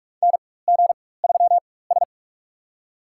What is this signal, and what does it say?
Morse code, heard as a keyed tone, about 46 wpm, spaced as for 21 wpm, 700 Hz. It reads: NG3S
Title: USA Call Signs - 4 Characters - 4x Character Spacing - 45wpm